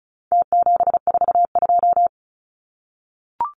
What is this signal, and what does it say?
Morse code, heard as a keyed tone, about 34 wpm, 700 Hz. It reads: T742
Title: Numbers - Contest Style - 4-Digit 35wpm